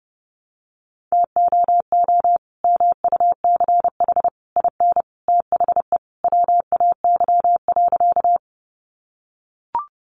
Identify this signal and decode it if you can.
Morse code, heard as a keyed tone, about 30 wpm, 700 Hz. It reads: TOO MUCH IN THE WAY.
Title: Sentences from Top 200 Words 30wpm